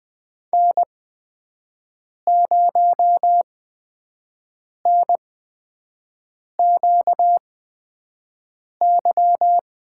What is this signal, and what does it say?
Morse code, heard as a keyed tone, about 20 wpm, spaced as for 5 wpm, 700 Hz. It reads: N0NQY
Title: USA Call Signs - 5 Characters - 8x Character Spacing - 20wpm